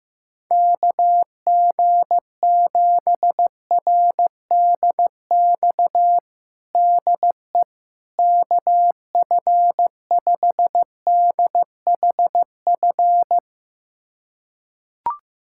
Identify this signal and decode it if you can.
Morse code, heard as a keyed tone, about 15 wpm, 700 Hz. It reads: KG7RDX DE KF5DHF
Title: USA Callsign DE Callsign - 6 Characters - 15wpm